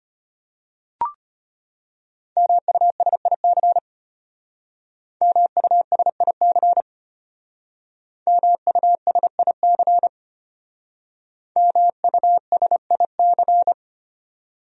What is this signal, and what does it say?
Morse code, heard as a keyed tone, about 34 wpm, 700 Hz. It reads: MUSIC MUSIC MUSIC MUSIC